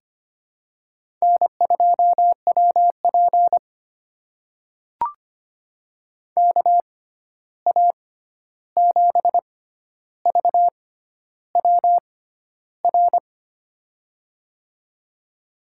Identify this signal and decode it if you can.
Morse code, heard as a keyed tone, about 25 wpm, spaced as for 11 wpm, 700 Hz. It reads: N2WP KA7VWR
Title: USA Call Signs - 6x Character Spacing - 25wpm